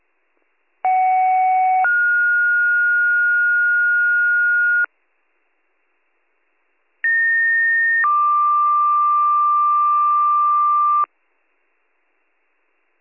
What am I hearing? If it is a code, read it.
Two-tone page, heard as a one-second tone, then a three-second tone, about 750 then 1430 Hz, about 1850 then 1180 Hz. It is heard from 0.8 to 11.1 s.